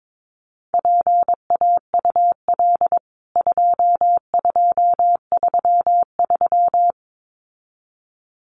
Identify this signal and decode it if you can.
Morse code, heard as a keyed tone, about 22 wpm, 700 Hz. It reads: PAUL 2233
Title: CWT Exchange 22wpm